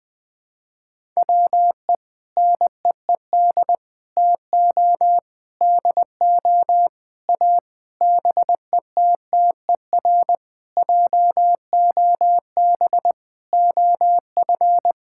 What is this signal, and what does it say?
Morse code, heard as a keyed tone, about 19 wpm, 700 Hz. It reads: WE NEED TO DO A BETTER JOB OF
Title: Sentences with 6-Word Phrases - Spoken Beginning - 20wpm